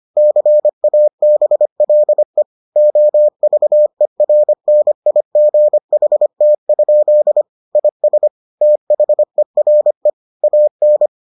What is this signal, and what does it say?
Morse code, heard as a keyed tone, about 25 wpm, 600 Hz. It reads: CABLE OVERNIGHT? IS THERE AN